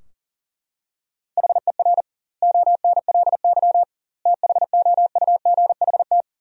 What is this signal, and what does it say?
Morse code, heard as a keyed tone, about 37 wpm, 700 Hz. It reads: HER ONLY THOUGHT
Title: 2x Word-Spacing - Sentences from Top 400 Words - 40wpm